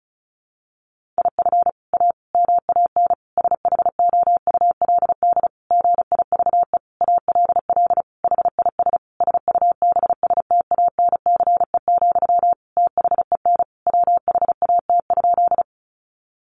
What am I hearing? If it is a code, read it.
Morse code, heard as a keyed tone, about 34 wpm, 700 Hz. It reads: IF A MAN SHOULD GIVE ALL HIS SUBSTANCE, THEN WHAT?
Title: Sentences from Top 1500 Words 35wpm